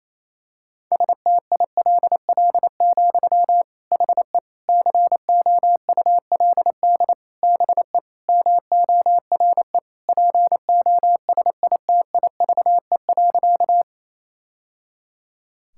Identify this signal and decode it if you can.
Morse code, heard as a keyed tone, about 28 wpm, 700 Hz. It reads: STILL, HE COULD BE MORE POSITIVE.
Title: Sentences from Top 1000 Words - Review Part 5 - 28wpm